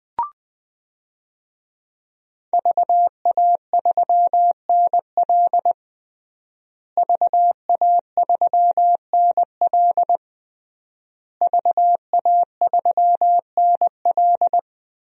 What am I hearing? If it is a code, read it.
Morse code, heard as a keyed tone, about 20 wpm, 700 Hz. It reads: VA3NL VA3NL VA3NL
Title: DXCC-100 Call Signs - 5 Characters - Repeated 3x - 20wpm